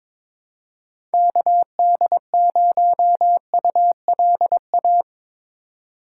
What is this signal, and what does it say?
Morse code, heard as a keyed tone, about 22 wpm, 700 Hz. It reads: KD0ULA
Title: USA Call Signs - 6 Characters - 2x Character Spacing - 22wpm